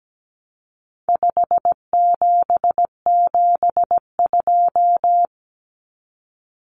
Morse code, heard as a keyed tone, about 17 wpm, 700 Hz. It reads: 5772
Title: Common Cut Numbers - 4-Digit 17wpm